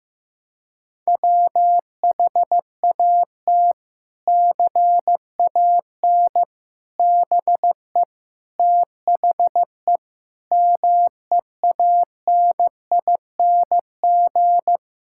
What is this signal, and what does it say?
Morse code, heard as a keyed tone, about 15 wpm, 700 Hz. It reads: WHAT CAN BE THE MEANING
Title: Sentences from Top 1000 Words - Review Part 6 - 15wpm